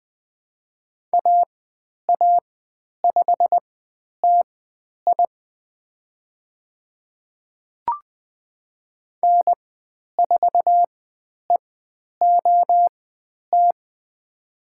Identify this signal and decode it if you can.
Morse code, heard as a keyed tone, about 20 wpm, spaced as for 10 wpm, 700 Hz. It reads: AA5TI N4EOT
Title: 102 Lesson 17g - Letter u - Call Signs 20wpm